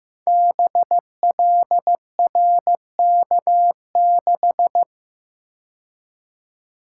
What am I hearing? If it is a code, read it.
Morse code, heard as a keyed tone, about 15 wpm, 700 Hz. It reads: BLRK6
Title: Code Groups - 5 Letter:Numbers - 15wpm